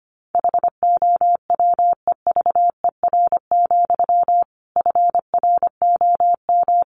Morse code, heard as a keyed tone, about 25 wpm, 700 Hz. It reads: HOWEVER, FROM
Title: Sentences from Top 1300 Words 25wpm